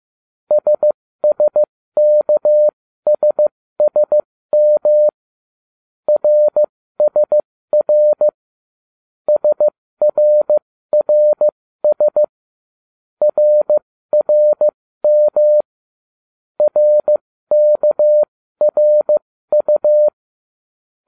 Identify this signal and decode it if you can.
Morse code, heard as a keyed tone, about 15 wpm, spaced as for 13 wpm, 600 Hz. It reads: SSKSSM RSR SRRS RRM RKRU